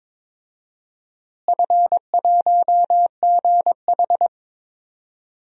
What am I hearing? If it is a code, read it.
Morse code, heard as a keyed tone, about 22 wpm, 700 Hz. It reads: F1GH
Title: DXCC-100 Call Signs - 4 Characters - Repeated 2x - 22wpm